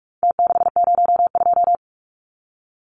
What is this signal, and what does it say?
Morse code, heard as a keyed tone, about 44 wpm, 700 Hz. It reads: T602
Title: Numbers - Contest Style - 4-Digit 45wpm